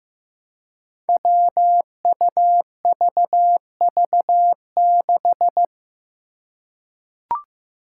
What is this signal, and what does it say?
Morse code, heard as a keyed tone, about 15 wpm, 700 Hz. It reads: WUVV6
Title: Five Challenging Letter-Number Combinations 15wpm